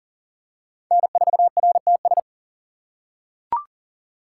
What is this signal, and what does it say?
Morse code, heard as a keyed tone, about 40 wpm, 700 Hz. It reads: N4RTS
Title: USA Call Signs - 6x Character Spacing - 40wpm